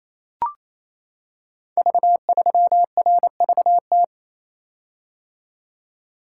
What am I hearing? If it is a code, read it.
Morse code, heard as a keyed tone, about 27 wpm, 700 Hz. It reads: V3RVT
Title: Code Groups - 5 Letter:Numbers - 28wpm